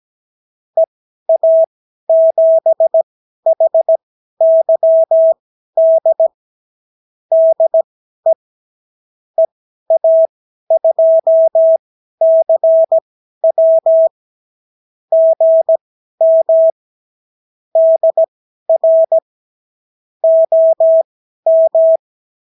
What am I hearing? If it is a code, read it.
Morse code, heard as a keyed tone, about 17 wpm, spaced as for 12 wpm, 650 Hz. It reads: EA7HYD DE EA2CW GM DR OM
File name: CW QSO libroCW4 18-8ppm